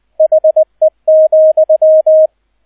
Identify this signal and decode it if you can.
Morse code, heard as a keyed tone, about 20 wpm, 630 Hz, with radio noise. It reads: HE,